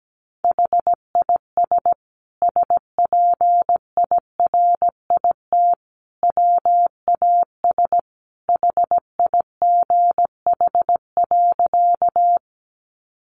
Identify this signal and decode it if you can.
Morse code, heard as a keyed tone, about 17 wpm, 700 Hz. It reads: HIS SPIRIT WAS HIGH.
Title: Sentences from Top 1400 Words 17wpm